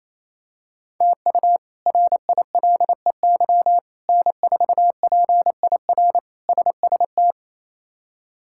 Morse code, heard as a keyed tone, about 28 wpm, 700 Hz. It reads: TU RILEY N4PIR SST
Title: K1USN SST QSO Practice - 28wpm